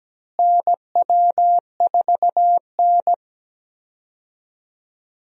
Morse code, heard as a keyed tone, about 17 wpm, 700 Hz. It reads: NW4N
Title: USA Call Signs - 4 Characters - Repeated 2x - 17wpm